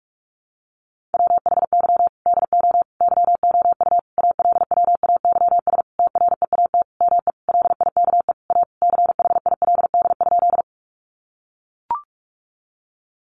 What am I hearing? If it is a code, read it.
Morse code, heard as a keyed tone, about 44 wpm, 700 Hz. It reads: WHY DO YOU ALWAYS TREAT ME LIKE A CHILD?
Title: Sentences from Top 1300 Words 45wpm